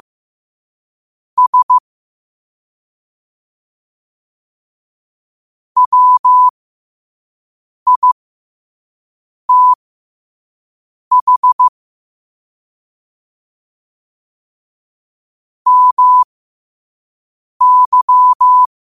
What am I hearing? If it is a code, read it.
Morse code, heard as a keyed tone, about 15 wpm, spaced as for 5 wpm, 1000 Hz. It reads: S WITH MY